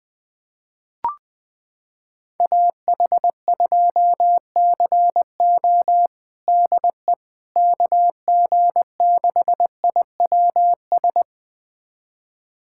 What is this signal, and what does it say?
Morse code, heard as a keyed tone, about 21 wpm, 700 Hz. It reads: AH2CO DE KG6IWS
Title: USA Callsign DE Callsign 20wpm